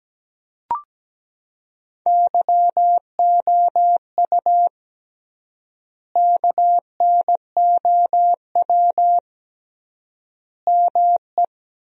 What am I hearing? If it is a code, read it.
Morse code, heard as a keyed tone, about 17 wpm, 700 Hz. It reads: YOU KNOW ME
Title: 3x Word-Spacing - Sentences from Top 400 Words - 17wpm